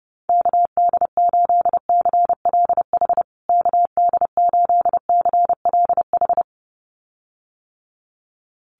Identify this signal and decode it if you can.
Morse code, heard as a keyed tone, about 30 wpm, 700 Hz. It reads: KD8CLH KD8CLH
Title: USA Call Signs Repeated 2x 30wpm